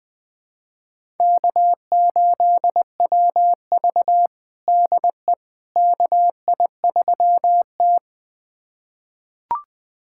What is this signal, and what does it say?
Morse code, heard as a keyed tone, about 20 wpm, 700 Hz. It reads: K8WV DE KI3T
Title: USA Callsign DE Callsign - 4 Characters - 20wpm